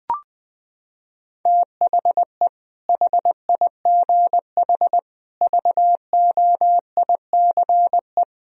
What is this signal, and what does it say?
Morse code, heard as a keyed tone, about 20 wpm, 700 Hz. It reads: THE HIGH VOICE
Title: Sentences from Top 1000 Words - Review Part 1 - 20wpm